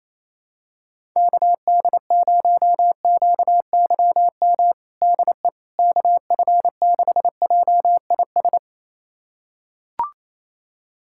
Morse code, heard as a keyed tone, about 28 wpm, 700 Hz. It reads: KD0QYM DE KF6JIS
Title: USA Callsign DE Callsign - 6 Characters - 28wpm